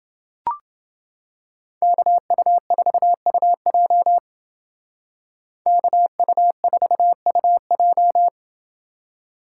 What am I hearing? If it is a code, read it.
Morse code, heard as a keyed tone, about 29 wpm, 700 Hz. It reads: KU4UJ KU4UJ